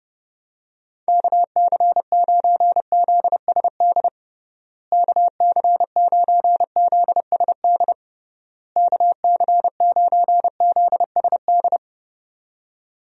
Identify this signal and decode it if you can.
Morse code, heard as a keyed tone, about 30 wpm, 700 Hz. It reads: KC9ZSD KC9ZSD KC9ZSD
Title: USA Call Signs - 6 Characters - Repeated 3x - 30wpm